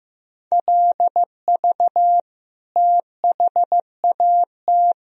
Morse code, heard as a keyed tone, about 15 wpm, 700 Hz. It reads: LV THAT
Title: Sentences from Top 400 Words and Abbreviations 15wpm